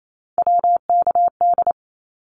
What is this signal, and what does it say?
Morse code, heard as a keyed tone, about 28 wpm, 700 Hz. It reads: WKD
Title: Abbreviations 28wpm